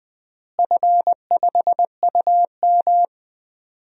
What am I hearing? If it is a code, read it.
Morse code, heard as a keyed tone, about 20 wpm, 700 Hz. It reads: F5UM
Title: DXCC-100 Call Signs - 4 Characters - Repeated 3x - 20wpm